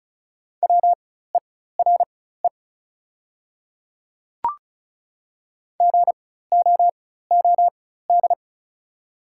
Morse code, heard as a keyed tone, about 36 wpm, spaced as for 16 wpm, 700 Hz. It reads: WERE GOOD
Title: 4x Character Spacing - Top 200 Words - 35wpm